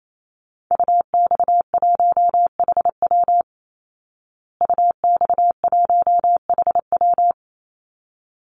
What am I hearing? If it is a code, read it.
Morse code, heard as a keyed tone, about 28 wpm, 700 Hz. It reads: UX1HW UX1HW
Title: Call Signs Repeated 3x - 28wpm